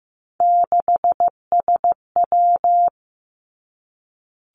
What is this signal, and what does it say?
Morse code, heard as a keyed tone, about 15 wpm, 700 Hz. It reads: 6SW
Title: Three Challenging Letter-Number Combinations 15wpm